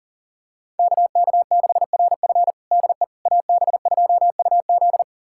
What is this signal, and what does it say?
Morse code, heard as a keyed tone, about 40 wpm, 700 Hz. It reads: KK6RF DE AB2UZ
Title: USA Callsign DE Callsign - 5 Characters - 40wpm